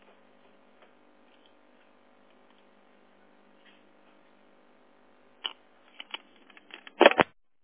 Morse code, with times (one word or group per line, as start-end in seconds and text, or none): none